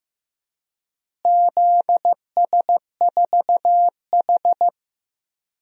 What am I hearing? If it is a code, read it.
Morse code, heard as a keyed tone, about 15 wpm, 700 Hz. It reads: ZS4H
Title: DXCC-100 Call Signs - 4 Characters - Repeated 3x - 15wpm